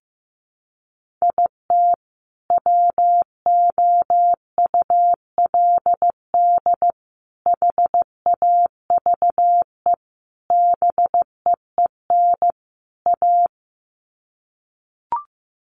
Morse code, heard as a keyed tone, about 15 wpm, 700 Hz. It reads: IT WOULD HAVE BEEN A
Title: Sets of 5 Words Encore 15wpm